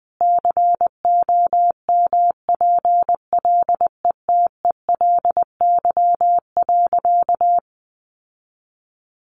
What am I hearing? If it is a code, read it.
Morse code, heard as a keyed tone, about 19 wpm, 700 Hz. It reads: COMPLETELY.
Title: Sentences from Top 1100 Words 20wpm